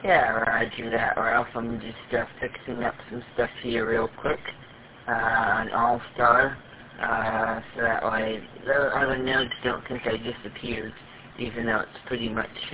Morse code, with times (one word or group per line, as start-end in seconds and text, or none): none